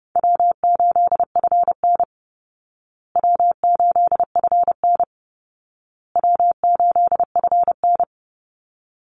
Morse code, heard as a keyed tone, about 30 wpm, 700 Hz. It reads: W8FN W8FN W8FN
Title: Easy Call Signs Repeated 3x - 30wpm